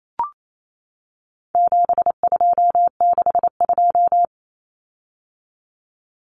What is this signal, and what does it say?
Morse code, heard as a keyed tone, about 28 wpm, 700 Hz. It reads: 7262
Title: Common Cut Numbers - 4-Digit 28wpm